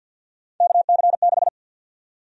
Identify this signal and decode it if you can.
Morse code, heard as a keyed tone, about 53 wpm, 700 Hz. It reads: KC6
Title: USA Call Sign Prefixes - 50wpm